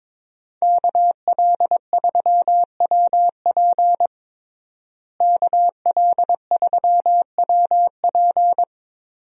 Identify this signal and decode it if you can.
Morse code, heard as a keyed tone, about 22 wpm, 700 Hz. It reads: KL3WP KL3WP
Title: USA Call Signs - 5 Characters - Repeated 2x - 22wpm